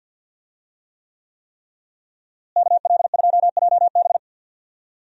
Morse code, heard as a keyed tone, about 46 wpm, 700 Hz. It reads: KL2JB
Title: USA Call Signs - 5 Characters - Repeated 3x - 50wpm